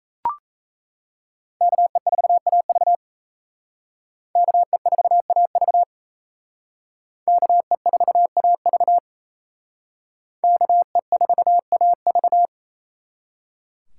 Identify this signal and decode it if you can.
Morse code, heard as a keyed tone, about 37 wpm, 700 Hz. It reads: KE4AV KE4AV KE4AV KE4AV